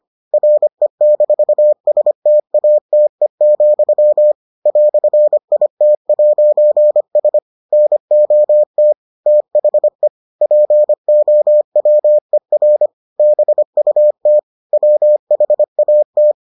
Morse code, heard as a keyed tone, about 25 wpm, 600 Hz. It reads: RE-STATE, "IT'S NOT THE POWER BUT WHAT